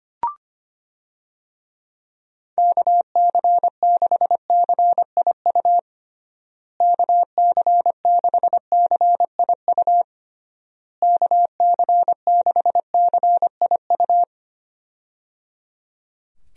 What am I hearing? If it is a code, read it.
Morse code, heard as a keyed tone, about 25 wpm, 700 Hz. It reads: KC6CIU KC6CIU KC6CIU
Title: USA Call Signs - 6 Characters - Repeated 3x - 25wpm